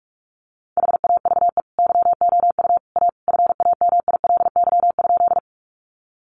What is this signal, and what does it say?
Morse code, heard as a keyed tone, about 44 wpm, 700 Hz. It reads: HAVE YOU A FAMILY?
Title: Sentences from Top 400 Words 45wpm